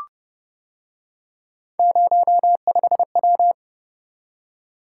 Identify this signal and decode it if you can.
Morse code, heard as a keyed tone, about 29 wpm, 700 Hz. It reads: 05W